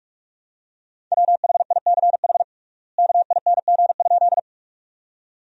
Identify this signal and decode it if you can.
Morse code, heard as a keyed tone, about 46 wpm, 700 Hz. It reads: WHICH KING?
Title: 3x Word-Spacing - Sentences from Top 400 Words - 45wpm